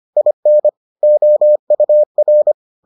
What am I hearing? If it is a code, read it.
Morse code, heard as a keyed tone, about 25 wpm, 600 Hz. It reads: IN OUR